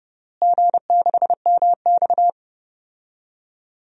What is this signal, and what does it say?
Morse code, heard as a keyed tone, about 32 wpm, 700 Hz. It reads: G6MX
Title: DXCC-100 Call Signs - 4 Characters - Repeated 3x - 30wpm